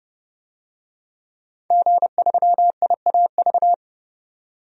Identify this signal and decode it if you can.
Morse code, heard as a keyed tone, about 30 wpm, 700 Hz. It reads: G3IAV
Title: DXCC-100 Call Signs - 5 Characters - 30wpm